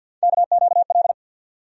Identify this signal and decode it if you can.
Morse code, heard as a keyed tone, about 48 wpm, 700 Hz. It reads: KQL
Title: USA Call Sign Postfixes - 3 Letters 50wpm